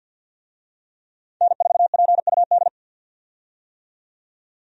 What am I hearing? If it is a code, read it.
Morse code, heard as a keyed tone, about 48 wpm, 700 Hz. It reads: N4PUD
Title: USA Call Signs - 5 Characters - 50wpm